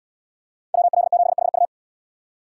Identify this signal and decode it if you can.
Morse code, heard as a keyed tone, about 68 wpm, 700 Hz. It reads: WL7HU